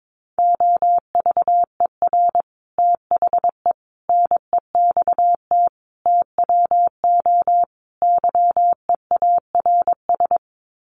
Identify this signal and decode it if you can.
Morse code, heard as a keyed tone, about 22 wpm, 700 Hz. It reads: OVER THE NEXT TWO YEARS
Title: Sets of 5 Words Encore2 22wpm